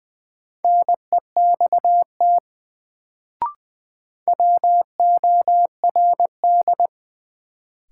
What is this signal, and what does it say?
Morse code, heard as a keyed tone, about 20 wpm, 700 Hz. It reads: NEXT WORD
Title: Sentences from Top 300 Words - New Word Drill - 20wpm